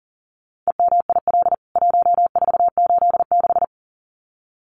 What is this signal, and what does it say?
Morse code, heard as a keyed tone, about 40 wpm, 700 Hz. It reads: EMIL 1486
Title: CWT Exchange 40wpm